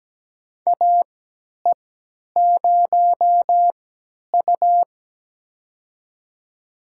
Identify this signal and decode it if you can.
Morse code, heard as a keyed tone, about 17 wpm, spaced as for 10 wpm, 700 Hz. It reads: AE0U
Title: USA Call Signs - 4 Characters - 3x Character Spacing - 17wpm